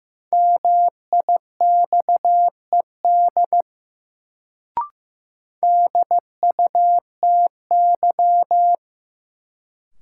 Morse code, heard as a keyed tone, about 15 wpm, 700 Hz. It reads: MIXED DUTY
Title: Sentences from Top 1500 Words - New Word Drill Alt - 15wpm